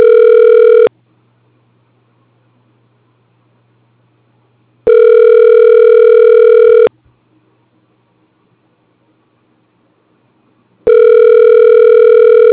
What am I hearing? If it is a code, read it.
Ringback tone (North America), 440 + 480 Hz.